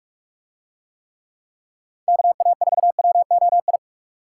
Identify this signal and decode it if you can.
Morse code, heard as a keyed tone, about 46 wpm, 700 Hz. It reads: KA4WOI